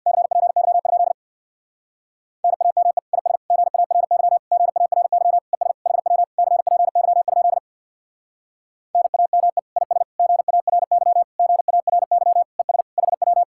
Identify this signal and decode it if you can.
Morse code, heard as a keyed tone, about 65 wpm, 700 Hz. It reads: CPY? NAME IS GARY GARY ES HW CPY? NAME IS GARY GARY ES HW